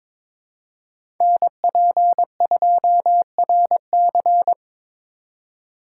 Morse code, heard as a keyed tone, about 21 wpm, 700 Hz. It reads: NP2RC
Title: USA Call Signs - 5 Characters - 3x Character Spacing - 22wpm